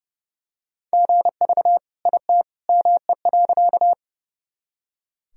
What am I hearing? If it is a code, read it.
Morse code, heard as a keyed tone, about 30 wpm, 700 Hz. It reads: GV IT ME.
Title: Sentences from Top 200 Words and Abbreviations 30wpm